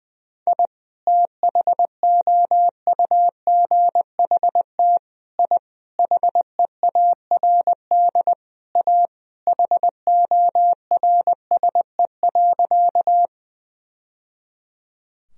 Morse code, heard as a keyed tone, about 21 wpm, 700 Hz. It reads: I THOUGHT I HEARD A HORSE.
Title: Sentences from Top 1000 Words - Review Part 1 - 20wpm